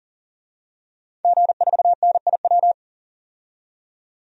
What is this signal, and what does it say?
Morse code, heard as a keyed tone, about 40 wpm, 700 Hz. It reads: G4NIW
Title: DXCC-100 Call Signs - 5 Characters - 40wpm